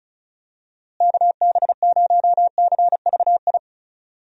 Morse code, heard as a keyed tone, about 34 wpm, 700 Hz. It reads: KB0CVI
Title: USA Call Signs - 6 Characters - Repeated 2x - 35wpm